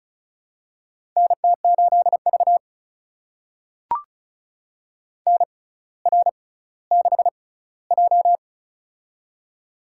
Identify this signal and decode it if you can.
Morse code, heard as a keyed tone, about 36 wpm, spaced as for 18 wpm, 700 Hz. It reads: NT8V NR6J
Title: USA Call Signs - 4 Characters - 6x Character Spacing - 35wpm